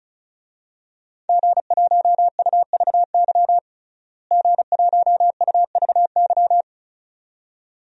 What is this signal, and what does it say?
Morse code, heard as a keyed tone, about 34 wpm, 700 Hz. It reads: G1UVY G1UVY
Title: DXCC-100 Call Signs - 5 Characters - Repeated 2x - 35wpm